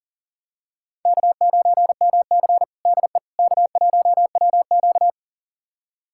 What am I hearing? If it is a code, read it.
Morse code, heard as a keyed tone, about 40 wpm, 700 Hz. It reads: K9MC DE K1WQ
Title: USA Callsign DE Callsign - 4 Characters - 40wpm